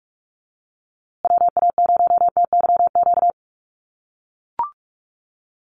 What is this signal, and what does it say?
Morse code, heard as a keyed tone, about 44 wpm, 700 Hz. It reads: WA0TYQ
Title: USA Call Signs Repeated 2x 45wpm